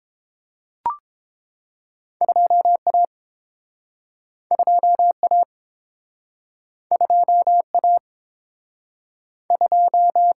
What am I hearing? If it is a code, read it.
Morse code, heard as a keyed tone, about 30 wpm, 700 Hz. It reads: 2A 2A 2A 2